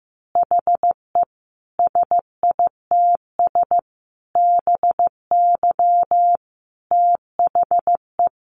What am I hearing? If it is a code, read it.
Morse code, heard as a keyed tone, about 15 wpm, 700 Hz. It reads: HE SITS BY THE
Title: Additional Easy Phrases 15wpm